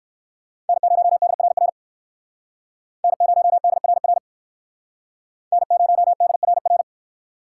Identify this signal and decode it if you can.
Morse code, heard as a keyed tone, about 68 wpm, 700 Hz. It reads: N0DRR N0DRR N0DRR